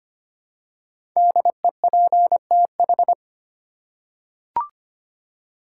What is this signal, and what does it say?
Morse code, heard as a keyed tone, about 25 wpm, 700 Hz. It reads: DEPTH
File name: 5 Letter Words - First 3 Letters Given - 25wpm